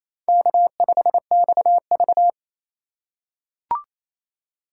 Morse code, heard as a keyed tone, about 28 wpm, 700 Hz. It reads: K5XV
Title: USA Call Signs - 3x Character Spacing - 28wpm